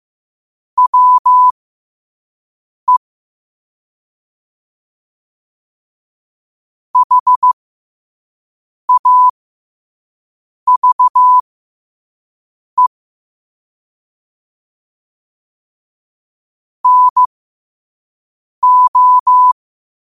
Morse code, heard as a keyed tone, about 15 wpm, spaced as for 5 wpm, 1000 Hz. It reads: WE HAVE NO
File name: qotdfiles / 5wpm